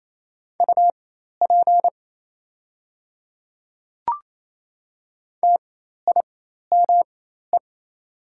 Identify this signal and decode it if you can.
Morse code, heard as a keyed tone, about 27 wpm, spaced as for 13 wpm, 690 Hz. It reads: UP TIME
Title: 4x Character Spacing - Top 100 Words - 28wpm